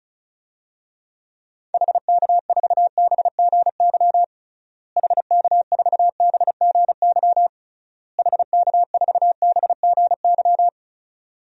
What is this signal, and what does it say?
Morse code, heard as a keyed tone, about 36 wpm, 700 Hz. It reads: HK4BGY HK4BGY HK4BGY
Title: DXCC-100 Call Signs - 6 Characters - Repeated 3x - 35wpm